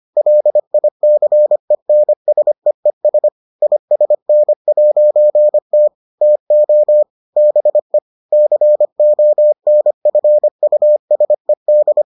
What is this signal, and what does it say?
Morse code, heard as a keyed tone, about 25 wpm, 600 Hz. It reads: LICENSEES ISN'T TO BE CONFUSED